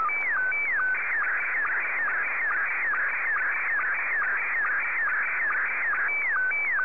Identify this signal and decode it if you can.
SSTV (slow-scan television), Scottie 1.
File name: sstv / S1-VK3DEK